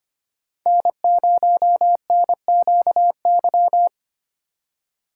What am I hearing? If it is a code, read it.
Morse code, heard as a keyed tone, about 25 wpm, 700 Hz. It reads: N0NQY
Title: USA Call Signs - 5 Characters - 2x Character Spacing - 25wpm